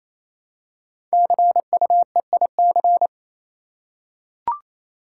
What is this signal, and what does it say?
Morse code, heard as a keyed tone, about 27 wpm, 700 Hz. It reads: CUEIC
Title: Code Groups - 5 Letter:Numbers - 28wpm